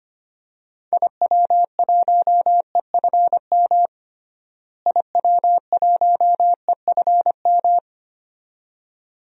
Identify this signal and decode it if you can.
Morse code, heard as a keyed tone, about 25 wpm, 700 Hz. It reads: IW1EFM IW1EFM
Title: DXCC-100 Call Signs - 6 Characters - Repeated 2x - 25wpm